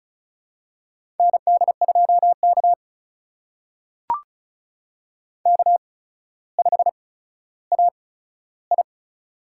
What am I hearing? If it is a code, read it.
Morse code, heard as a keyed tone, about 35 wpm, spaced as for 15 wpm, 700 Hz. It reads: ND2K K5AI
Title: USA Call Signs - 4 Characters - 8x Character Spacing - 35wpm